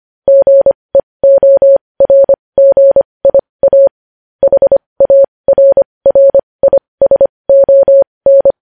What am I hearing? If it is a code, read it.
Morse code, heard as a keyed tone, about 25 wpm, spaced as for 20 wpm, 550 Hz. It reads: GEORGIA HARRISON
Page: News Headlines in Morse Code at 20 WPM